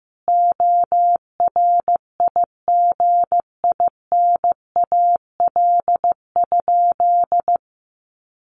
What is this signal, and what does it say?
Morse code, heard as a keyed tone, about 15 wpm, 700 Hz. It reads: ORIGINAL?
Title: Sentences from Top 1500 Words 15wpm